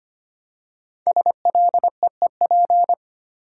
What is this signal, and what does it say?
Morse code, heard as a keyed tone, about 25 wpm, 700 Hz. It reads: SLEEP
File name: 5 Letter Words - First 3 Letters Given - 25wpm